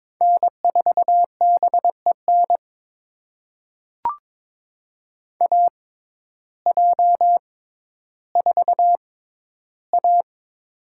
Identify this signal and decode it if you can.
Morse code, heard as a keyed tone, about 22 wpm, spaced as for 12 wpm, 700 Hz. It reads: N4BEN AJ4A
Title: USA Call Signs - 6x Character Spacing - 22wpm